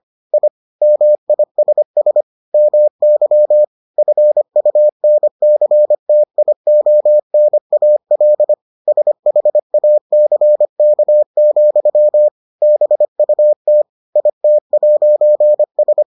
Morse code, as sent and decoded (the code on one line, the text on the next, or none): .. / -- .. ... ... / -- -.-- / ..-. ..- -. -.-. - .. --- -. .- .-.. / ... .... .- -.-. -.- --..-- / -... ..- - / .. - .----. ...
I MISS MY FUNCTIONAL SHACK, BUT IT'S